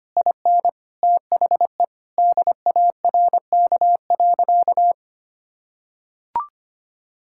Morse code, heard as a keyed tone, about 25 wpm, 700 Hz. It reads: IN THE DARK.
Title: Sentences from Top 1600 Words 25wpm